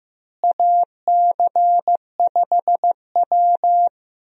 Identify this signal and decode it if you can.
Morse code, heard as a keyed tone, about 15 wpm, 700 Hz. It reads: AC5W
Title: USA Call Signs - 4 Characters - Repeated 2x - 15wpm